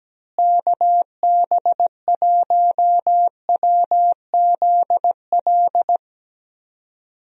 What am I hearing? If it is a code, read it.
Morse code, heard as a keyed tone, about 17 wpm, 700 Hz. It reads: KB1WZL
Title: USA Call Signs - 6 Characters - Repeated 2x - 17wpm